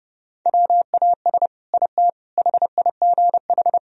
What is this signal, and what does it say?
Morse code, heard as a keyed tone, about 30 wpm, 700 Hz. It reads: WAS IT HIGH
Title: Sentences from Top 1000 Words - Review Part 4 - 30wpm